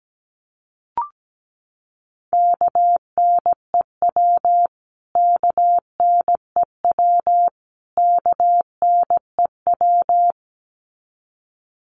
Morse code, heard as a keyed tone, about 17 wpm, 700 Hz. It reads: KNEW KNEW KNEW
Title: Top 500 Words Repeated 17wpm